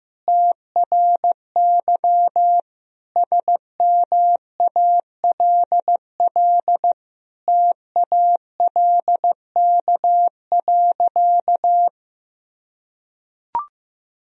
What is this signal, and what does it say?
Morse code, heard as a keyed tone, about 15 wpm, 700 Hz. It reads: TRY SMALL TALK.
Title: Sentences from Top 1000 Words - Review Part 1 - 15wpm